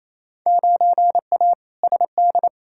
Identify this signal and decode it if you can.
Morse code, heard as a keyed tone, about 28 wpm, 700 Hz. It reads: 9A SD
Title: Field Day Exchanges Encore - 28wpm